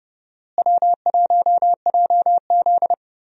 Morse code, heard as a keyed tone, about 31 wpm, 700 Hz. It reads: W1JZ
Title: USA Call Signs - 4 Characters - Repeated 3x - 30wpm